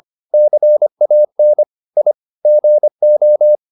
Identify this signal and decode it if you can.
Morse code, heard as a keyed tone, about 25 wpm, 600 Hz. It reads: CAN I GO